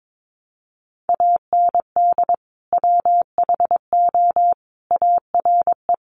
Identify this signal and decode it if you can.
Morse code, heard as a keyed tone, about 22 wpm, 700 Hz. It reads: AND WHO ARE
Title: Sentences from Top 100 Words 22wpm